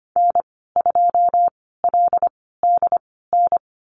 Morse code, heard as a keyed tone, about 25 wpm, spaced as for 16 wpm, 690 Hz. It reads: N2LDN
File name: Lesson 15g - Number 5 - Call Signs 25wpm